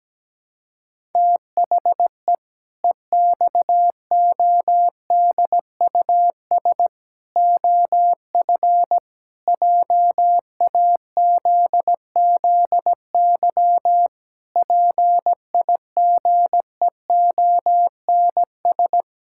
Morse code, heard as a keyed tone, about 17 wpm, 700 Hz. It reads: THE EXODUS OF JAZZY PIGEONS
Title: Pangram Challenge v2 - 17wpm